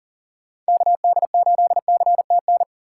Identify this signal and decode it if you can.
Morse code, heard as a keyed tone, about 37 wpm, 700 Hz. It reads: KD8CTN
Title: USA Call Signs - 6 Characters - 3x Character Spacing - 40wpm